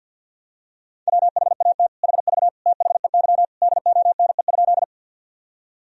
Morse code, heard as a keyed tone, about 45 wpm, 700 Hz. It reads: WHAT HV THEY DONE?
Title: Sentences from Top 500 Words and Abbreviations 50wpm